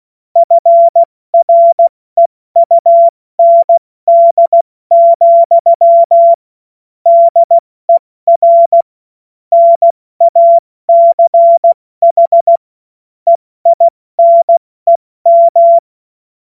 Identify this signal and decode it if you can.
Morse code, heard as a keyed tone, about 16 wpm, 680 Hz. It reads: FREUND, DER NACH EINEM
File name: Morsekurs / Witze / Witz 20. Dezember 2022 70 F80